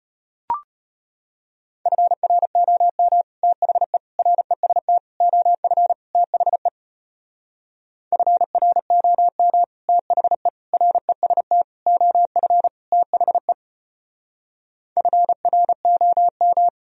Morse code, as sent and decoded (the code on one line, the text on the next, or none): ..-. .-. --- -- / - .... / .-. ... - / --- ..-. / - .... / ..-. .-. --- -- / - .... / .-. ... - / --- ..-. / - .... / ..-. .-. --- --
FROM THE REST OF THE FROM THE REST OF THE FROM